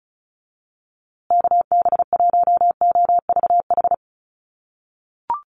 Morse code, heard as a keyed tone, about 34 wpm, 700 Hz. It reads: KB1OVH
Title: USA Call Signs Repeated 2x 35wpm